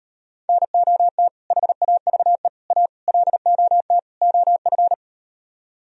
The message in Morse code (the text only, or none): NOT HAVE A LOT OF